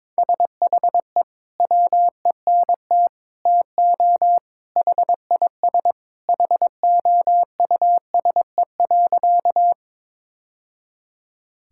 Morse code, heard as a keyed tone, about 21 wpm, 700 Hz. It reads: SHE WENT TO HIS HOUSE.
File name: practice / 22wpm / Sentences with 2-Word Phrases - 22wpm